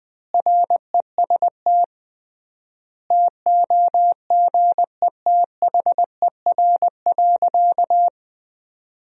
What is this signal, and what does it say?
Morse code, heard as a keyed tone, about 20 wpm, 700 Hz. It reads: REST TOGETHER.
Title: 3x Word-Spacing - Sentences from Top 500 Words - 20wpm